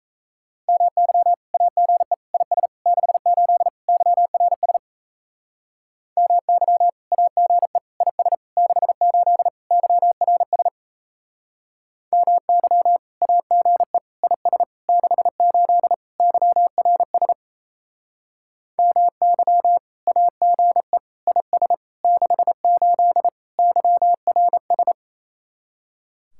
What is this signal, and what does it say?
Morse code, heard as a keyed tone, about 35 wpm, 700 Hz. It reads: MY AGE IS 68 YRS MY AGE IS 68 YRS MY AGE IS 68 YRS MY AGE IS 68 YRS
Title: Speed-Racing Common QSO Phrases Encore - 28wpm